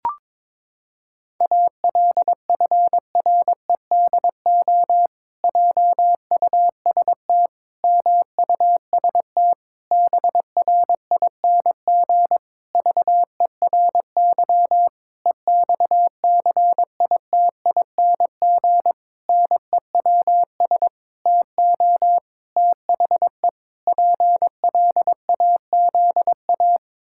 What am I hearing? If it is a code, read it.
Morse code, heard as a keyed tone, about 22 wpm, 700 Hz. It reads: ALFREDO JUST MUST BRING VERY EXCITING NEWS TO THE PLAZA
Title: Pangram Challenge v2 - 22wpm